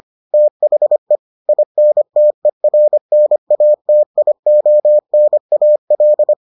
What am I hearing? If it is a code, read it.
Morse code, heard as a keyed tone, about 25 wpm, 600 Hz. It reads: THE INTERNATIONAL